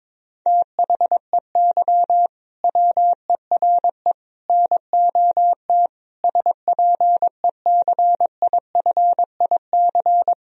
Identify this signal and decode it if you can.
Morse code, heard as a keyed tone, about 22 wpm, 700 Hz. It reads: THEY WERE NOT SPECIFIC
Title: Sentences from Top 1000 Words - Review Part 3 - 22wpm